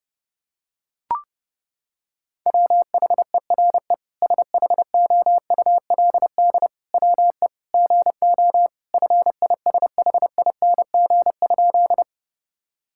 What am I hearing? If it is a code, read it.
Morse code, heard as a keyed tone, about 30 wpm, 700 Hz. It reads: WHERE SHOULD WE GO FISHING?
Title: Sentences from Top 1000 Words - Review Part 5 - 30wpm